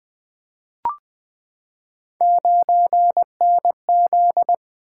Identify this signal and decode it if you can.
Morse code, heard as a keyed tone, about 20 wpm, 700 Hz. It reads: 9NZ
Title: Three Letter-Number Combinations - 20wpm